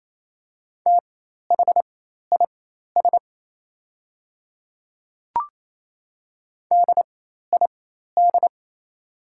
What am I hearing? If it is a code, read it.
Morse code, heard as a keyed tone, about 28 wpm, spaced as for 13 wpm, 690 Hz. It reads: THIS DID
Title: 4x Character Spacing - Top 100 Words - 28wpm